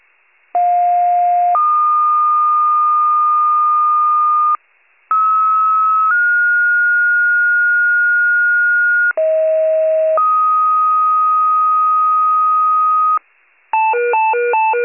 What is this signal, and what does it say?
Two-tone page, heard as a one-second tone, then a three-second tone, about 710 then 1220 Hz, about 1290 then 1430 Hz, about 640 then 1250 Hz. It is heard from 0.5 to 13.2 s.
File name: wcalert